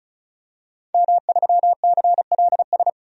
Morse code, heard as a keyed tone, about 36 wpm, 700 Hz. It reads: M3CLS
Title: DXCC-100 Call Signs - 5 Characters - 35wpm